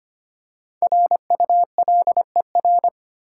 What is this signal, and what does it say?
Morse code, heard as a keyed tone, about 25 wpm, 690 Hz. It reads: RULER
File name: practice / 25wpm / Easy Words with Hints - 25wpm